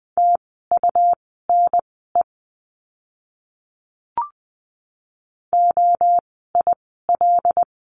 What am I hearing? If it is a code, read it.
Morse code, heard as a keyed tone, about 20 wpm, spaced as for 15 wpm, 700 Hz. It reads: TUNE OIL
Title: 101 Lesson 17f - Letter u - Words 20wpm